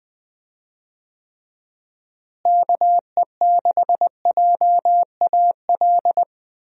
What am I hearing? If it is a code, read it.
Morse code, heard as a keyed tone, about 20 wpm, 700 Hz. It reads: KE6JAL
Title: USA Call Signs - 6 Characters - Repeated 2x - 20wpm